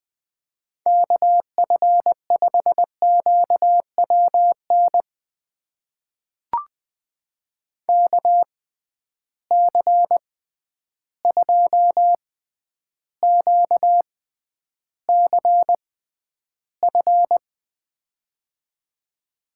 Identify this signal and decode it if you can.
Morse code, heard as a keyed tone, about 20 wpm, spaced as for 10 wpm, 700 Hz. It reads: KF5QWN KC2QCF
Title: USA Call Signs - 6 Characters - 6x Character Spacing - 20wpm